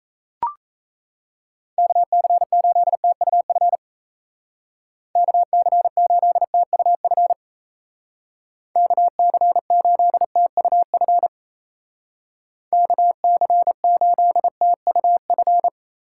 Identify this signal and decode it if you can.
Morse code, heard as a keyed tone, about 35 wpm, 700 Hz. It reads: KC8TUF KC8TUF KC8TUF KC8TUF
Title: Speed Racing - USA 2x3 Call Signs 28wpm